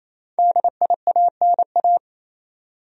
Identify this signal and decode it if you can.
Morse code, heard as a keyed tone, about 28 wpm, 700 Hz. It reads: DIANA
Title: USA Common Names - 28wpm